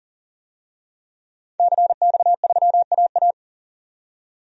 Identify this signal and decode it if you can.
Morse code, heard as a keyed tone, about 40 wpm, 700 Hz. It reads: CX3AA
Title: DXCC-100 Call Signs - 5 Characters - 40wpm